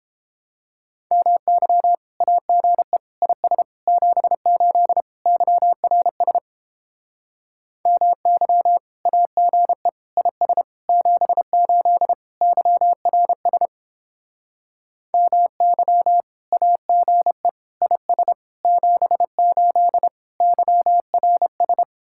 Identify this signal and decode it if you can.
Morse code, heard as a keyed tone, about 31 wpm, 700 Hz. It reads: MY AGE IS 78 YRS MY AGE IS 78 YRS MY AGE IS 78 YRS